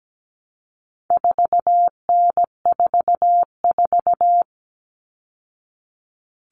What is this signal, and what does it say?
Morse code, heard as a keyed tone, about 17 wpm, 700 Hz. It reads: 4N44
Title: All Cut Numbers - 4-Digit 17wpm